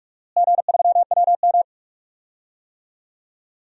Morse code, heard as a keyed tone, about 42 wpm, 700 Hz. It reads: G3WM
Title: DXCC-100 Call Signs - 4 Characters - Repeated 2x - 45wpm